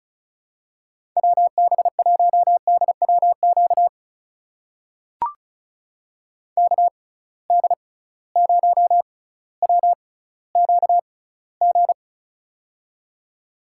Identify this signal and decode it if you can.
Morse code, heard as a keyed tone, about 36 wpm, spaced as for 18 wpm, 700 Hz. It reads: WB1DWQ KD0WQG
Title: USA Call Signs - 6 Characters - 6x Character Spacing - 35wpm